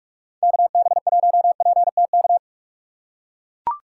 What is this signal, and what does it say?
Morse code, heard as a keyed tone, about 40 wpm, 700 Hz. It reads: KB1PTK